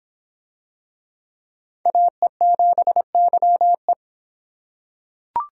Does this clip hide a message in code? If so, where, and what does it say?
Yes: Morse code from 1.9 to 3.9 s, heard as a keyed tone, about 23 wpm, 700 Hz. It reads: AE7YE